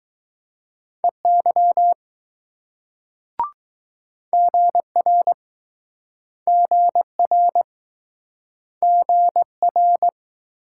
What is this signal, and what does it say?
Morse code, heard as a keyed tone, about 20 wpm, 700 Hz. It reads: EY GR GR GR